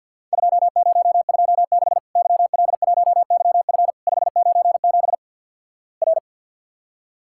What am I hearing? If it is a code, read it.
Morse code, heard as a keyed tone, about 50 wpm, 700 Hz. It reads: 2026 YL1YF 597 R
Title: Worked All Europe DX Contest - QTC Exchange - 50wpm